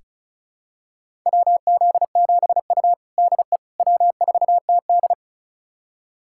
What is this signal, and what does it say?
Morse code, heard as a keyed tone, about 36 wpm, 700 Hz. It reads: WZ7U DE W4TD
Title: USA Callsign DE Callsign - 4 Characters - 35wpm